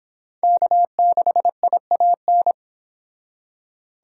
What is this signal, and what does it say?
Morse code, heard as a keyed tone, about 26 wpm, 700 Hz. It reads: K6IAN